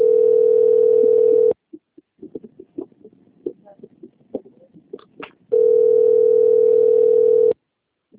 Ringback tone (North America), 440 + 480 Hz.